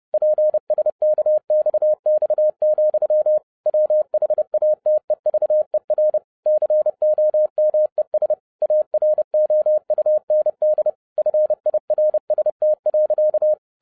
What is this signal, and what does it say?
Morse code, heard as a keyed tone, about 30 wpm, 610 Hz. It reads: PSKXX, WHATEVER COMES AROUND FIRST.